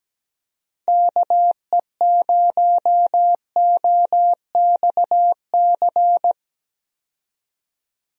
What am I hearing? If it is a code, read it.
Morse code, heard as a keyed tone, about 17 wpm, 700 Hz. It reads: KE0OXC
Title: USA Call Signs - 6 Characters - 17wpm